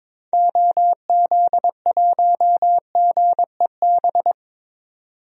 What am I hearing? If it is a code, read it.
Morse code, heard as a keyed tone, about 22 wpm, 700 Hz. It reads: OZ1GEB
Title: DXCC-100 Call Signs - 6 Characters - 22wpm